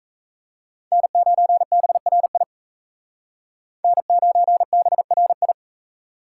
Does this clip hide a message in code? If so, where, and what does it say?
Yes: Morse code from 0.9 to 5.5 s, heard as a keyed tone, about 36 wpm, 700 Hz. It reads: N9BRI N9BRI